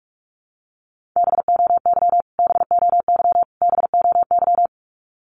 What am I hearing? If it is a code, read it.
Morse code, heard as a keyed tone, about 46 wpm, 700 Hz. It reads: BOY BOY BOY
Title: Top 300 Words Repeated 45wpm